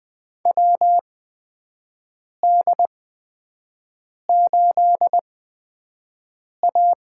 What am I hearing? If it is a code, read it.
Morse code, heard as a keyed tone, about 20 wpm, spaced as for 5 wpm, 700 Hz. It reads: WD8A
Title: USA Call Signs - 6 Characters - 8x Character Spacing - 20wpm